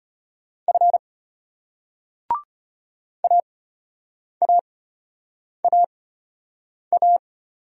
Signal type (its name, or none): Morse code